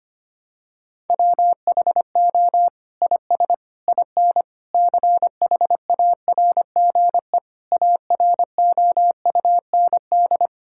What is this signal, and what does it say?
Morse code, heard as a keyed tone, about 25 wpm, 700 Hz. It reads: WHO IS IN CHARGE AROUND
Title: Sentences from Top 1000 Words - Review Part 6 - 25wpm